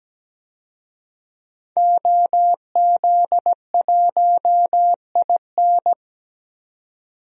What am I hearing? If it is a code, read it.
Morse code, heard as a keyed tone, about 17 wpm, 700 Hz. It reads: OZ1IN
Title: DXCC-100 Call Signs - 5 Characters - 17wpm